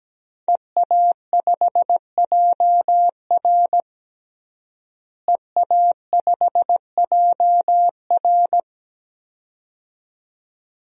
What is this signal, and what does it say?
Morse code, heard as a keyed tone, about 17 wpm, 700 Hz. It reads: EA5JR EA5JR
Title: DXCC-100 Call Signs - 5 Characters - Repeated 2x - 17wpm